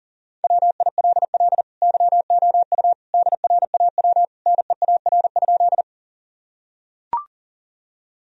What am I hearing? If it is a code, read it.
Morse code, heard as a keyed tone, about 40 wpm, 700 Hz. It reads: WILL YOU DRAW NEAR?
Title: Sentences from Top 1000 Words - Review Part 1 - 40wpm